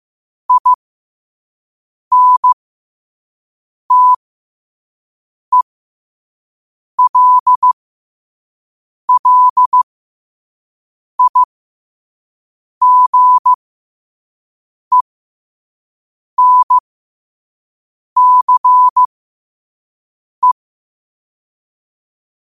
Morse code, heard as a keyed tone, about 15 wpm, spaced as for 5 wpm, 1000 Hz. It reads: INTELLIGENCE
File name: qotdfiles / 5wpm